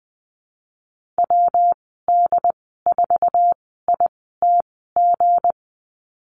Morse code, heard as a keyed tone, about 20 wpm, spaced as for 15 wpm, 700 Hz. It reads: WD4ITG
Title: 167 Lesson 27g - Number 7 - Call Signs 20wpm